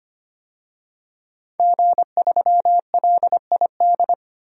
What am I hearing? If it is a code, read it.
Morse code, heard as a keyed tone, about 25 wpm, 700 Hz. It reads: G3LID
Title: DXCC-100 Call Signs - 5 Characters - 25wpm